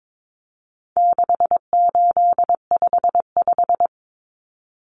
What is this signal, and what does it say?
Morse code, heard as a keyed tone, about 22 wpm, 700 Hz. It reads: 6855
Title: Common Cut Numbers - 4-Digit 22wpm